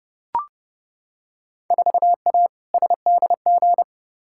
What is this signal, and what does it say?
Morse code, heard as a keyed tone, about 30 wpm, 700 Hz. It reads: 4A SDG
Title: Field Day Exchanges - 30wpm